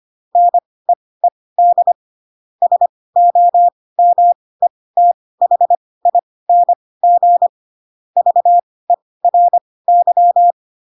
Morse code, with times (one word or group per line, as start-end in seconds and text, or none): 0.4-1.9 NEED
2.6-7.5 SOMETHING
8.2-10.5 VERY